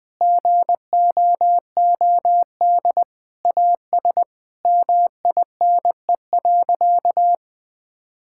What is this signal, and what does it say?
Morse code, heard as a keyed tone, about 20 wpm, 700 Hz. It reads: GOOD AS MINE.
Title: Sentences with 7-Word Phrases - 20wpm